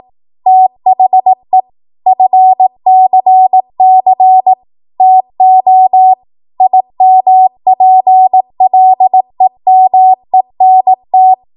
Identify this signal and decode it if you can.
Morse code, heard as a keyed tone, about 18 wpm, 750 Hz. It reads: THE FCC TO IMPLEMENT
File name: Morse / 18WPM